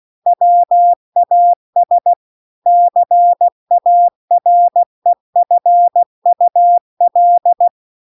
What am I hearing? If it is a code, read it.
Morse code, heard as a keyed tone, about 16 wpm, 700 Hz. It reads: WAS CAREFUL
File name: CW-The-Big-Twoheaded-River-2-16WPM